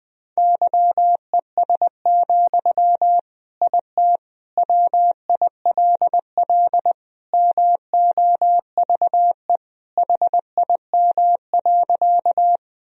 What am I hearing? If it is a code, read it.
Morse code, heard as a keyed tone, about 20 wpm, 700 Hz. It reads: YES, IT WILL MOVE HIM.
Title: Sentences from Top 1000 Words - Review Part 1 - 20wpm